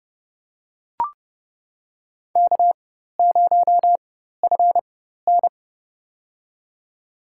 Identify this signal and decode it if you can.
Morse code, heard as a keyed tone, about 31 wpm, spaced as for 14 wpm, 700 Hz. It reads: K0FN
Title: USA Call Signs - 4x Character Spacing - 30wpm